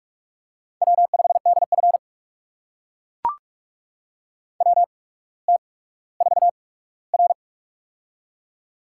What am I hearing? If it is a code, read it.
Morse code, heard as a keyed tone, about 44 wpm, spaced as for 19 wpm, 700 Hz. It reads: W5DF WT4R
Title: USA Call Signs - 4 Characters - 8x Character Spacing - 45wpm